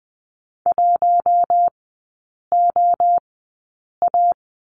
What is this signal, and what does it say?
Morse code, heard as a keyed tone, about 20 wpm, spaced as for 8 wpm, 700 Hz. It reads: 1OA
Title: 074 Lesson 13e - Letter L - Three Characters 20wpm